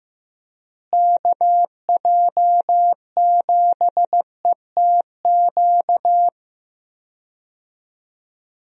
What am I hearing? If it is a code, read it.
Morse code, heard as a keyed tone, about 16 wpm, 700 Hz. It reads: KJ7ETQ
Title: USA Call Signs - 6 Characters - Repeated 2x - 15wpm